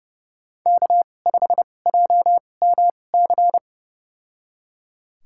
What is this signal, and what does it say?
Morse code, heard as a keyed tone, about 32 wpm, spaced as for 22 wpm, 700 Hz. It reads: K5JMC
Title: USA Call Signs - 5 Characters - 2x Character Spacing - 30wpm